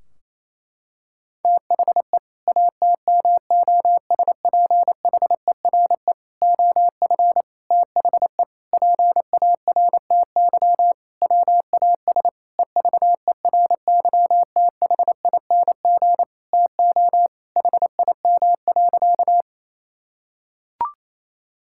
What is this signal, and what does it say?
Morse code, heard as a keyed tone, about 28 wpm, 700 Hz. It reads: THE ATMOSPHERE OF THE PARTY WAS EVERYTHING TO HIM.
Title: Sentences from Top 1600 Words 28wpm